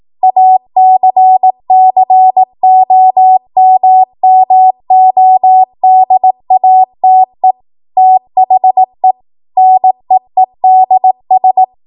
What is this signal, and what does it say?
Morse code, heard as a keyed tone, about 18 wpm, 750 Hz. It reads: ACCOMMODATE THE NEEDS
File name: Morse / 18WPM